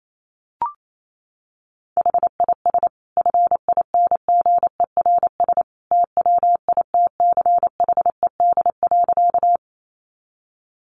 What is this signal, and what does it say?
Morse code, heard as a keyed tone, about 28 wpm, 700 Hz. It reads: HIS FINGERS TWITCHED.
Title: A Sound Of Thunder 28wpm